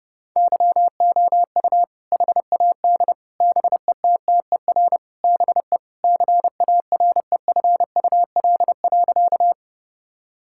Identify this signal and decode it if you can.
Morse code, heard as a keyed tone, about 30 wpm, 700 Hz. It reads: YOU HAD BETTER BE CAREFUL.
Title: Sentences from Top 1600 Words 30wpm